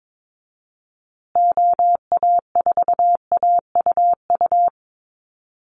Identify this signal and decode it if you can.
Morse code, heard as a keyed tone, about 22 wpm, 700 Hz. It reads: OA4AUU
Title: USA Call Signs Repeated 2x 22wpm